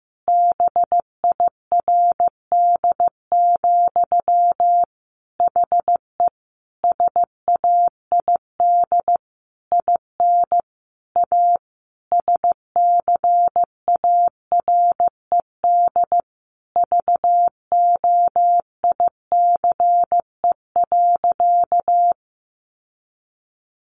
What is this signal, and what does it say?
Morse code, heard as a keyed tone, about 15 wpm, 700 Hz. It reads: BIRD, HE SAID IN A SCARED VOICE.
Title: Sentences from Top 700 Words 15wpm